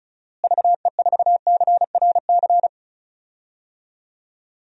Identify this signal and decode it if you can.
Morse code, heard as a keyed tone, about 36 wpm, 700 Hz. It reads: VE4CRC
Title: DXCC-100 Call Signs - 6 Characters - Repeated 2x - 35wpm